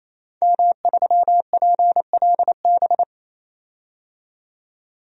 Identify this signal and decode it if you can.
Morse code, heard as a keyed tone, about 28 wpm, 700 Hz. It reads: M3PLB
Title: DXCC-100 Call Signs - 5 Characters - 28wpm